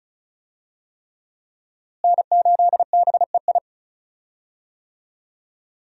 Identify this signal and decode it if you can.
Morse code, heard as a keyed tone, about 36 wpm, 700 Hz. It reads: N8BEI
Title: USA Call Signs - 5 Characters - Repeated 3x - 35wpm